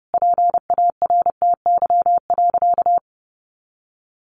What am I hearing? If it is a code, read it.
Morse code, heard as a keyed tone, about 30 wpm, 700 Hz. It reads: PARTY.
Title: Sentences from Top 1500 Words 30wpm